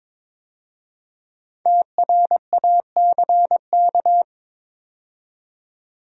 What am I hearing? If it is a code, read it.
Morse code, heard as a keyed tone, about 22 wpm, 700 Hz. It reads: TRACK
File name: 5 Letter Words - First 3 Letters Given - 22wpm